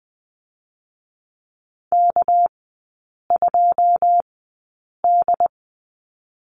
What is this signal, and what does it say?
Morse code, heard as a keyed tone, about 19 wpm, 700 Hz. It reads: K 2 D
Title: 205 Lesson 33e - Letter j - Three Characters 20wpm